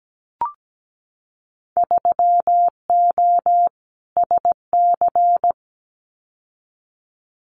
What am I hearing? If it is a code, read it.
Morse code, heard as a keyed tone, about 17 wpm, 700 Hz. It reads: 3O SC
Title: Winter Field Day Exchanges Encore 17wpm